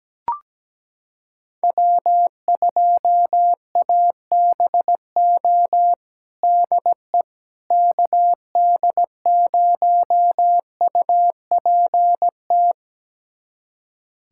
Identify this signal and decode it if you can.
Morse code, heard as a keyed tone, about 17 wpm, 700 Hz. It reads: W2ABO DE KD0UPT
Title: USA Callsign DE Callsign 17wpm